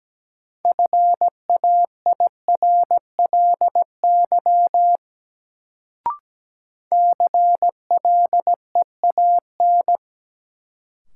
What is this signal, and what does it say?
Morse code, heard as a keyed tone, about 17 wpm, 700 Hz. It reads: FAIRLY CLEAN
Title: Sentences from Top 1100 Words - New Word Drill - 17wpm